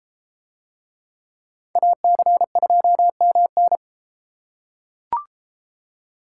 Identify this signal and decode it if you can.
Morse code, heard as a keyed tone, about 31 wpm, 700 Hz. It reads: AC2MN